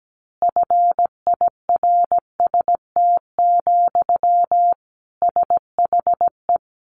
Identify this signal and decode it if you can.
Morse code, heard as a keyed tone, about 17 wpm, 700 Hz. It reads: FIRST, SHE
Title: Easy Phrases 17wpm